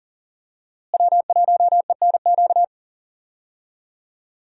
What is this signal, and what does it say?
Morse code, heard as a keyed tone, about 40 wpm, 700 Hz. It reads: W1ENQ
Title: USA Call Signs - 5 Characters - Repeated 3x - 40wpm